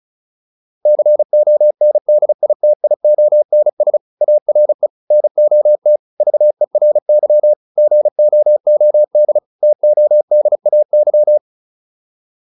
Morse code, heard as a keyed tone, about 34 wpm, 600 Hz. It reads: CONDITIONS ARE NOT VERY GOOD TODAY
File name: QSO407 35